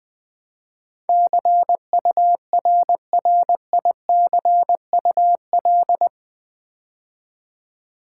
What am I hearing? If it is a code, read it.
Morse code, heard as a keyed tone, about 20 wpm, 700 Hz. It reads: CURRICUL
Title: Build A Word - 4 Letters At A Time - 10-Letter Words - 20wpm